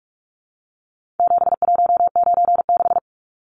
Morse code, heard as a keyed tone, about 44 wpm, 700 Hz. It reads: 7196
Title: Numbers - 4-Digit 45wpm